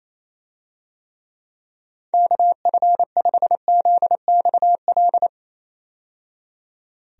Morse code, heard as a keyed tone, about 28 wpm, 700 Hz. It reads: KF5ZXL
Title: USA Call Signs - 6 Characters - 28wpm